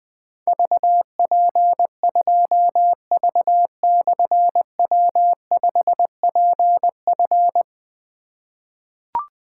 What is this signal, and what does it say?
Morse code, heard as a keyed tone, about 20 wpm, 700 Hz. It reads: VP2V/W5PF
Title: Difficult Call Signs 20wpm